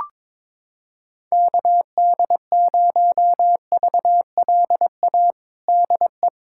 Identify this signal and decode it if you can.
Morse code, heard as a keyed tone, about 22 wpm, 700 Hz. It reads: KD0VLA DE
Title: USA Callsign DE Callsign - 6 Characters - 22wpm